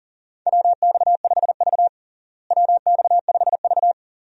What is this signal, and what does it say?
Morse code, heard as a keyed tone, about 40 wpm, 700 Hz. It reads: WX5V WX5V
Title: USA Call Signs - 4 Characters - Repeated 2x - 40wpm